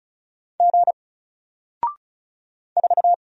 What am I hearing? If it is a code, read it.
Morse code, heard as a keyed tone, about 35 wpm, 700 Hz. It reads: G 4
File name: Single-Letter-Number-35wpm